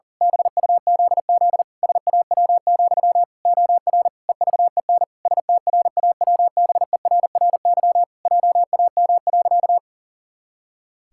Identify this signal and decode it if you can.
Morse code, heard as a keyed tone, about 40 wpm, 700 Hz. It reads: BUZZ SAW, OR EVEN STRAWBERRY JAM.
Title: Pangram Challenge v2 - 40wpm